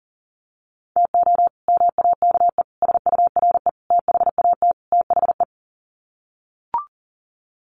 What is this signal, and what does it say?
Morse code, heard as a keyed tone, about 40 wpm, 700 Hz. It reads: TO MAKE SURE THAT THE
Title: Sets of 5 Words 40wpm